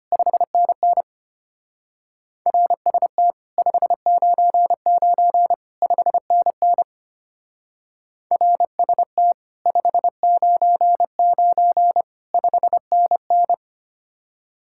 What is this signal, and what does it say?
Morse code, heard as a keyed tone, about 31 wpm, 700 Hz. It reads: 5NN RST 599 5NN RST 599 5NN
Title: Speed-Racing Common QSO Phrases - 25wpm